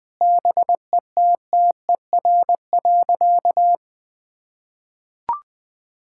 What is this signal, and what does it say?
Morse code, heard as a keyed tone, about 20 wpm, 700 Hz. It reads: BETTER.
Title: Sentences with 2-Word Phrases - 20wpm